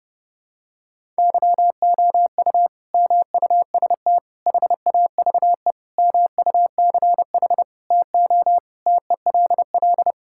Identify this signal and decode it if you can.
Morse code, heard as a keyed tone, about 30 wpm, 700 Hz. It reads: YOU MUST HAVE MUCH TO TELL
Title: Sentences from Top 1600 Words 30wpm